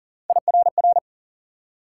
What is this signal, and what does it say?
Morse code, heard as a keyed tone, about 55 wpm, spaced as for 48 wpm, 700 Hz. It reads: IRR
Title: Speed-Racing Multi-letter Phonemes - 40wpm